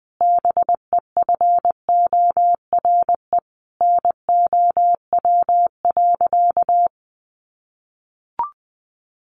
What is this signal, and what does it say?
Morse code, heard as a keyed tone, about 19 wpm, 700 Hz. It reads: BEFORE NOW.
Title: Sentences from Top 1300 Words 20wpm